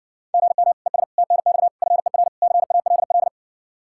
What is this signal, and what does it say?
Morse code, heard as a keyed tone, about 60 wpm, 700 Hz. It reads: GM ES TNX FER CALL